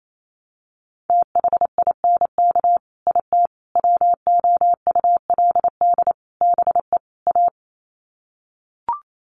Morse code, heard as a keyed tone, about 28 wpm, 700 Hz. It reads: THINK IT WOULD BE A
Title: Sets of 5 Words Encore2 28wpm